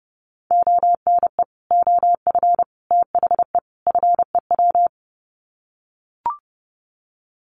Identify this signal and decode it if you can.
Morse code, heard as a keyed tone, about 30 wpm, 700 Hz. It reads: ONE OF THE FEW
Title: Sets of 4 Words 30wpm